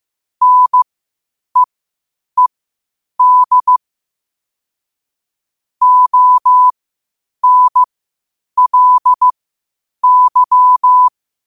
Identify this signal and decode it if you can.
Morse code, heard as a keyed tone, about 15 wpm, spaced as for 8 wpm, 1000 Hz. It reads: NEED ONLY
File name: qotdfiles / 8wpm